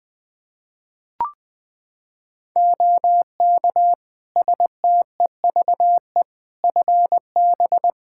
Morse code, heard as a keyed tone, about 20 wpm, 700 Hz. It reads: OK STEVE FB
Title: Practice QSO - Exchange 2 - Info - 20wpm